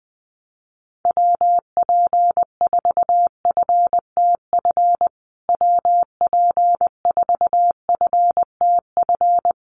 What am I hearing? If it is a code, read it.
Morse code, heard as a keyed tone, about 20 wpm, 700 Hz. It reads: WP4FTF WP4FTF
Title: USA Call Signs Repeated 3x 20wpm